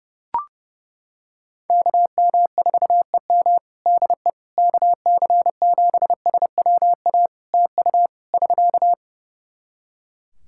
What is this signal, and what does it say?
Morse code, heard as a keyed tone, about 28 wpm, 700 Hz. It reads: KM4EM DE KC7SWA TU <SK>
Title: Practice QSO - Exchange 4 - Ending - 30wpm